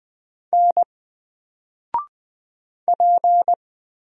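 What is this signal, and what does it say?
Morse code, heard as a keyed tone, about 20 wpm, 700 Hz. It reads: N P
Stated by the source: Single letter and number Morse drill at 20 wpm, from N4LCD.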